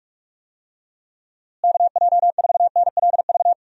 Morse code, heard as a keyed tone, about 43 wpm, 700 Hz. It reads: KJ4NLV
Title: USA Call Signs - 6 Characters - Repeated 2x - 45wpm